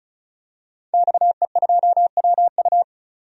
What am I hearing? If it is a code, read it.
Morse code, heard as a keyed tone, about 34 wpm, 700 Hz. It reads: XE2WU
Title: DXCC-100 Call Signs - 5 Characters - 35wpm